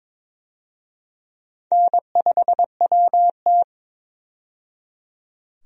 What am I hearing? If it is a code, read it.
Morse code, heard as a keyed tone, about 22 wpm, 700 Hz. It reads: N5WT
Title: USA Call Signs - 4 Characters - 22wpm